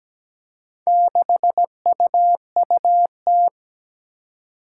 Morse code, heard as a keyed tone, about 17 wpm, 700 Hz. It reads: 6UUT
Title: Four Challenging Letter-Number Combinations 17wpm